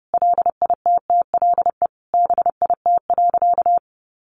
Morse code, heard as a keyed tone, about 30 wpm, 700 Hz. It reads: LITTLE BIT.
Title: Sentences from Top 900 Words 30wpm